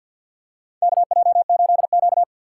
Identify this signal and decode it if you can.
Morse code, heard as a keyed tone, about 53 wpm, 700 Hz. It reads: KJ8Q
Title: USA Call Signs - 3x Character Spacing - 50wpm